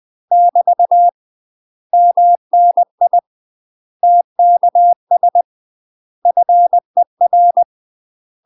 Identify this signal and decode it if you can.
Morse code, heard as a keyed tone, about 20 wpm, 700 Hz. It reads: <BT> MNI TKS FER
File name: UK MASTER QSOs TRAINING/20WPM